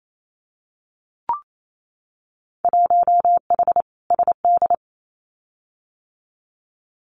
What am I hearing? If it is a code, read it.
Morse code, heard as a keyed tone, about 28 wpm, 700 Hz. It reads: 1H SD
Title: Winter Field Day Exchanges 28wpm